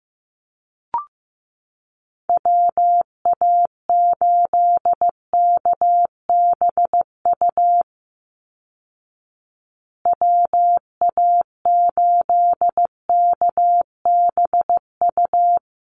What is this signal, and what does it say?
Morse code, heard as a keyed tone, about 15 wpm, 700 Hz. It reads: WA8KBU WA8KBU
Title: Call Signs Repeated 3x - 15wpm